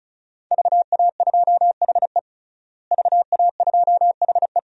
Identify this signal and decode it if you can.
Morse code, heard as a keyed tone, about 34 wpm, 700 Hz. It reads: VA2HE VA2HE
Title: DXCC-100 Call Signs - 5 Characters - Repeated 2x - 35wpm